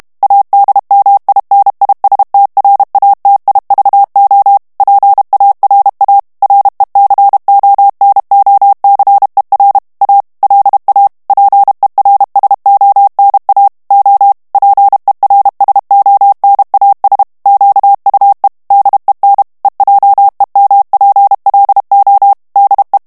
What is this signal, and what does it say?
Morse code, heard as a keyed tone, about 32 wpm, 800 Hz. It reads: ADMINISTRATIVO PARA RECONOCER A LA PERSONA O PERSONAS QUE DEN EJEMPLO DE
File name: morse mp3 / ESPAGNOL TEXTE EN CLAIR 1 VITESSE 30 MOTS MINUTES